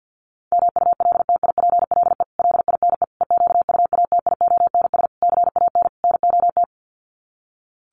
Morse code, heard as a keyed tone, about 50 wpm, 700 Hz. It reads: MULTIPLE LINE EQUATIONS CAN NOT
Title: Sets of 5 Words 50wpm